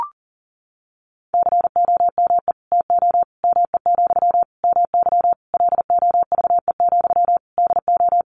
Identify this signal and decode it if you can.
Morse code, heard as a keyed tone, about 40 wpm, 700 Hz. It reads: COME TO ME, MY LOVE, DO
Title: Sentences from Top 1500 Words 40wpm